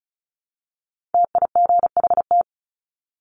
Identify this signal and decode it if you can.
Morse code, heard as a keyed tone, about 34 wpm, 700 Hz. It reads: TIGHT
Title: Top 1751-2000 Words 35wpm